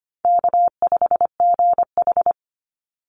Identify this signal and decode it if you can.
Morse code, heard as a keyed tone, about 25 wpm, 700 Hz. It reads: K5GH
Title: Easy Call Signs Repeated 3x - 25wpm